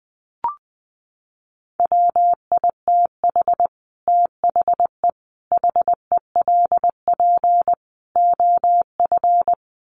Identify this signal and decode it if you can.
Morse code, heard as a keyed tone, about 20 wpm, 700 Hz. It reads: WITH THE HELP OF
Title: Sets of 4 Words 20wpm